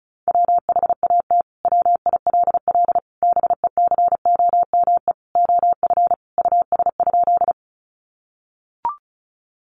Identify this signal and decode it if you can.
Morse code, heard as a keyed tone, about 34 wpm, 700 Hz. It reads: WHAT WILL BECOME OF US?
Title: Sentences from Top 600 Words 35wpm